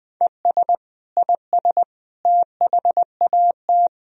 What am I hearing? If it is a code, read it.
Morse code, heard as a keyed tone, about 21 wpm, 700 Hz. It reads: ES IS THAT
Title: Sentences from Top 100 Words and Abbreviations 20wpm